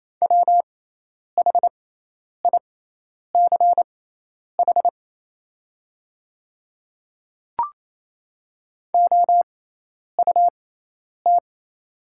Morse code, heard as a keyed tone, about 28 wpm, spaced as for 10 wpm, 700 Hz. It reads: WHICH OUT
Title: 6x Character Spacing - Top 300 Words - 28wpm